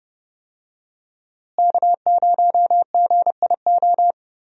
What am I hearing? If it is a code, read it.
Morse code, heard as a keyed tone, about 30 wpm, 700 Hz. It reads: K0GIO
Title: USA Call Signs - 5 Characters - Repeated 3x - 30wpm